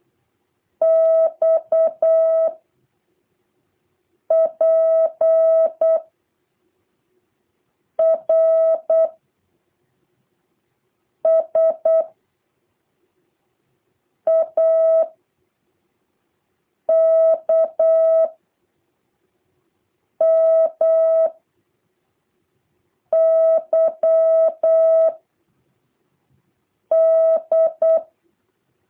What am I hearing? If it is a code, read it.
Morse code, heard as a keyed tone, about 8 wpm, 650 Hz. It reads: X P R S A K M Y D
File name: jh8cbh / 2023cw